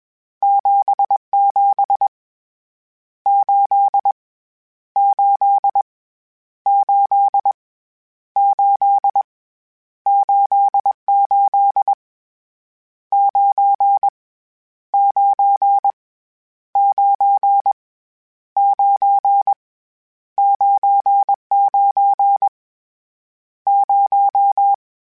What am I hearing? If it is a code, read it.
Morse code, heard as a keyed tone, about 21 wpm, 800 Hz. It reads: 77 8 8 8 8 88 9 9 9 9 99 0
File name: Misc / Misc-mp3s / Numb @20